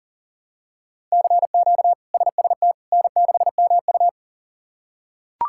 Morse code, heard as a keyed tone, about 40 wpm, 700 Hz. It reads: CQ SST N6MU
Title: K1USN SST QSO Practice - 40wpm